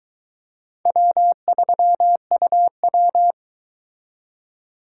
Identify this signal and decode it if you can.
Morse code, heard as a keyed tone, about 23 wpm, 700 Hz. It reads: W3UW